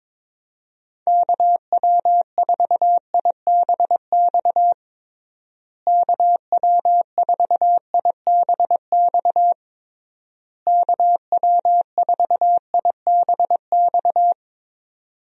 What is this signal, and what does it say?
Morse code, heard as a keyed tone, about 22 wpm, 700 Hz. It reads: KW4IBX KW4IBX KW4IBX
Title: USA Call Signs - 6 Characters - Repeated 3x - 22wpm